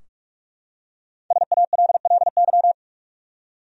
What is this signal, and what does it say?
Morse code, heard as a keyed tone, about 46 wpm, 700 Hz. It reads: SALLY
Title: USA Common Names - Max 5 Letters - 45wpm